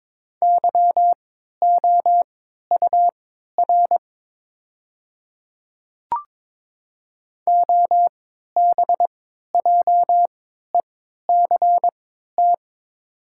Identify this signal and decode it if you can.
Morse code, heard as a keyed tone, about 22 wpm, spaced as for 13 wpm, 700 Hz. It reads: YOUR OBJECT
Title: 3x Character Spacing - Top 500 Words - 22wpm